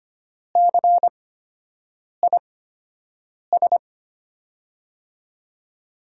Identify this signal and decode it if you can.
Morse code, heard as a keyed tone, about 25 wpm, spaced as for 7 wpm, 700 Hz. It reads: CIS
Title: USA Call Signs - 5 Characters - 8x Character Spacing - 25wpm